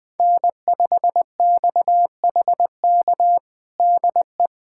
Morse code, heard as a keyed tone, about 20 wpm, 700 Hz. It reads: N5XHK DE
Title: USA Callsign DE Callsign - 5 Characters - 20wpm